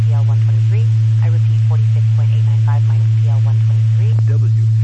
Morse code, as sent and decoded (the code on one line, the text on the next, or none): none